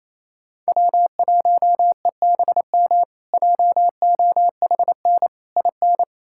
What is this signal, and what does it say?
Morse code, heard as a keyed tone, about 28 wpm, 700 Hz. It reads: W1EBM JOHN IN
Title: K1USN SST QSO Practice - 28wpm